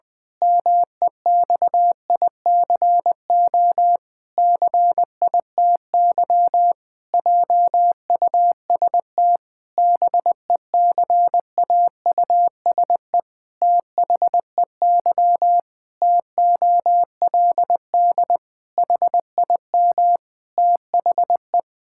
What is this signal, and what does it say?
Morse code, heard as a keyed tone, about 20 wpm, 700 Hz. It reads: MEXICO CITY JUST BECAUSE THEY TOLD HIM THE
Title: Pangram Challenge v2 - 20wpm